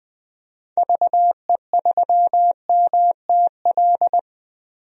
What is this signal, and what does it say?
Morse code, heard as a keyed tone, about 20 wpm, 700 Hz. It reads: VE3MTL
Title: DXCC-100 Call Signs - 6 Characters - Repeated 2x - 20wpm